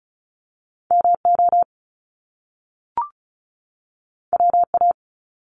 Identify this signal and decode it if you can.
Morse code, heard as a keyed tone, about 36 wpm, 700 Hz. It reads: MO WA
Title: Top Two Letter Combinations in Words 35wpm